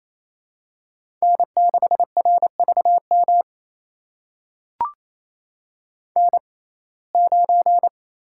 Morse code, heard as a keyed tone, about 28 wpm, spaced as for 20 wpm, 700 Hz. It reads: N6RVM N9
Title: USA Call Signs - 5 Characters - 6x Character Spacing - 28wpm